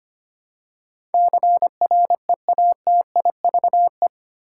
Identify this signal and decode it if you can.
Morse code, heard as a keyed tone, about 25 wpm, 700 Hz. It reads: CREATIVE
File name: 8 Letter Words - First 3 Letters Given - 25wpm